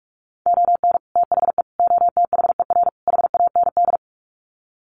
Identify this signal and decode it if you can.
Morse code, heard as a keyed tone, about 44 wpm, 700 Hz. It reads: ON THE OTHER HAND
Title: Sets of 4 Words 45wpm